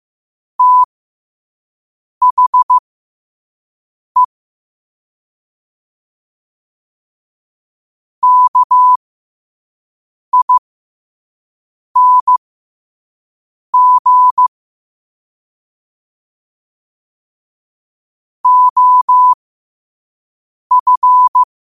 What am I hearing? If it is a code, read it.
Morse code, heard as a keyed tone, about 15 wpm, spaced as for 5 wpm, 1000 Hz. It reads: THE KING OF
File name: qotdfiles / 5wpm